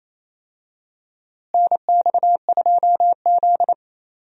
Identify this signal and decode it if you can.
Morse code, heard as a keyed tone, about 27 wpm, 700 Hz. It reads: NX2Z
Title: USA Call Signs - 4 Characters - Repeated 3x - 28wpm